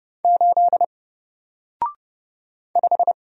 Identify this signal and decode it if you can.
Morse code, heard as a keyed tone, about 29 wpm, 700 Hz. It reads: 8 5
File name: Single-Letter-Number-30wpm